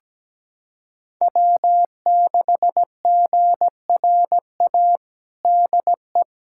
Morse code, heard as a keyed tone, about 17 wpm, 700 Hz. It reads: W6GRA DE
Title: USA Callsign DE Callsign 17wpm